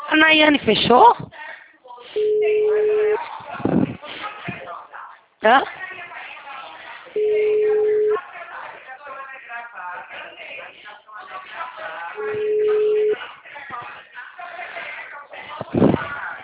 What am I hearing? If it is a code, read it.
Ringback tone (Europe, 425 Hz), with speech.
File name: out- -2007- - - .56455